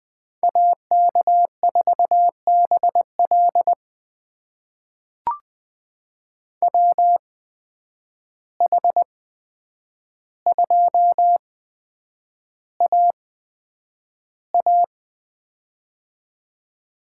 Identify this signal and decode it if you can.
Morse code, heard as a keyed tone, about 21 wpm, spaced as for 9 wpm, 700 Hz. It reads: AK4BL WH2AA
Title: USA Call Signs - 5 Characters - 8x Character Spacing - 20wpm